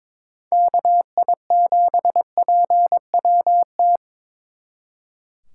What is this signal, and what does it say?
Morse code, heard as a keyed tone, about 26 wpm, 700 Hz. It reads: KI7PWT
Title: Speed Racing - USA 2x3 Call Signs 22wpm